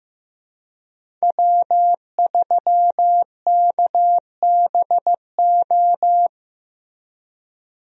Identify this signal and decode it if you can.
Morse code, heard as a keyed tone, about 15 wpm, 700 Hz. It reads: W3KBO
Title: USA Call Signs - 5 Characters - Repeated 3x - 15wpm